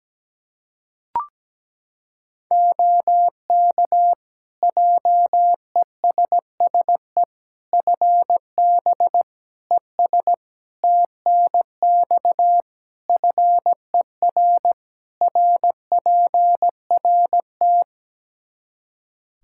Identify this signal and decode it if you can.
Morse code, heard as a keyed tone, about 17 wpm, 700 Hz. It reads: OK JESSE FB ES TNX FER RPRT
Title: Practice QSO - Exchange 2 - Info - 17wpm